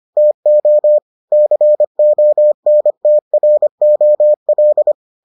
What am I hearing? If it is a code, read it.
Morse code, heard as a keyed tone, about 25 wpm, 600 Hz. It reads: TO CONTROL